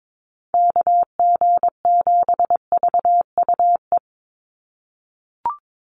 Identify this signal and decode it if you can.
Morse code, heard as a keyed tone, about 22 wpm, 700 Hz. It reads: KG7VUE
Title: USA Call Signs 22wpm